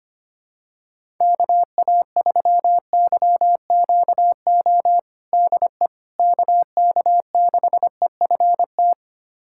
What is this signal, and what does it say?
Morse code, heard as a keyed tone, about 25 wpm, 700 Hz. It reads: KA3YQO DE KK6EFT
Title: USA Callsign DE Callsign - 6 Characters - 25wpm